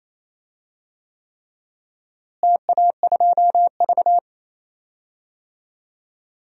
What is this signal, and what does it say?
Morse code, heard as a keyed tone, about 27 wpm, 700 Hz. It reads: TA2V
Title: DXCC-100 Call Signs - 4 Characters - Repeated 3x - 28wpm